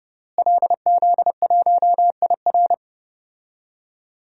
Morse code, heard as a keyed tone, about 29 wpm, 700 Hz. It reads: LZ1IR
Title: DXCC-100 Call Signs - 5 Characters - Repeated 3x - 30wpm